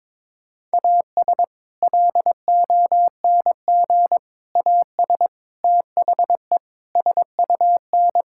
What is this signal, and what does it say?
Morse code, heard as a keyed tone, about 22 wpm, 700 Hz. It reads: AS LONG AS THE SUN
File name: practice / 22wpm / Sentences with 4-Word Phrases - 22wpm